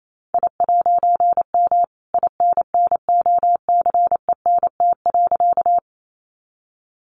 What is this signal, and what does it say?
Morse code, heard as a keyed tone, about 28 wpm, 700 Hz. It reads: I'M INNOCENT.
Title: A Sound Of Thunder 28wpm